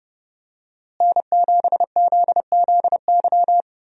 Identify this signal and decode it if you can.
Morse code, heard as a keyed tone, about 32 wpm, 700 Hz. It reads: N7ZZY
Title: USA Call Signs - 5 Characters - Repeated 3x - 30wpm